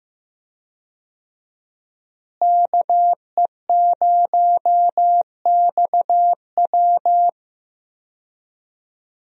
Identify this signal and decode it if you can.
Morse code, heard as a keyed tone, about 15 wpm, 700 Hz. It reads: KE0XW
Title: USA Call Signs - 5 Characters - Repeated 2x - 15wpm